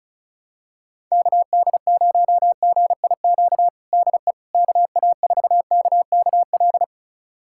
Morse code, heard as a keyed tone, about 34 wpm, 700 Hz. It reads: KD0GIQ DE KA4KKL
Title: USA Callsign DE Callsign - 6 Characters - 35wpm